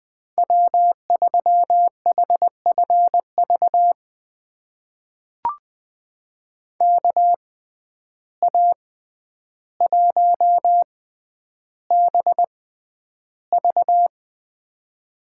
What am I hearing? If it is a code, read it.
Morse code, heard as a keyed tone, about 21 wpm, spaced as for 10 wpm, 700 Hz. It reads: W3HFV KA1BV
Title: USA Call Signs - 5 Characters - 6x Character Spacing - 20wpm